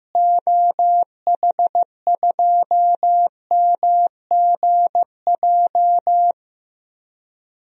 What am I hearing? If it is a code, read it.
Morse code, heard as a keyed tone, about 16 wpm, 700 Hz. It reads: OH2MGJ
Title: DXCC-100 Call Signs - 6 Characters - Repeated 3x - 15wpm